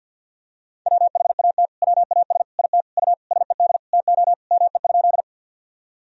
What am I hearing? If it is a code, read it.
Morse code, heard as a keyed tone, about 46 wpm, 700 Hz. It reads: WHAT WAS IT U SED TO ME?
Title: Sentences from Top 200 Words and Abbreviations 50wpm